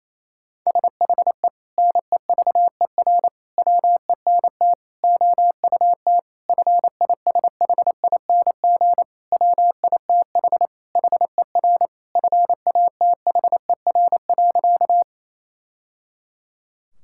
Morse code, heard as a keyed tone, about 28 wpm, 700 Hz. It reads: SHE NEVER WENT OUT FISHING WITH HER FATHER.
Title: Sentences from Top 1000 Words - Review Part 5 - 28wpm